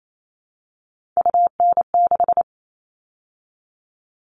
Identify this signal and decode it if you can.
Morse code, heard as a keyed tone, about 28 wpm, 700 Hz. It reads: UN6
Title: All Cut Numbers - 3-Digit 28wpm